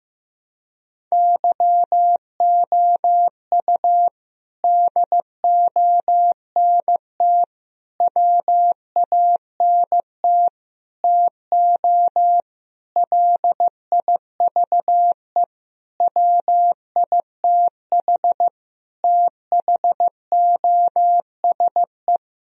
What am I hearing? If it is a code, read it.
Morse code, heard as a keyed tone, about 15 wpm, 700 Hz. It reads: YOU DONT WANT TO LIVE WITH THOSE
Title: Sentences from Top 1000 Words - Review Part 1 - 15wpm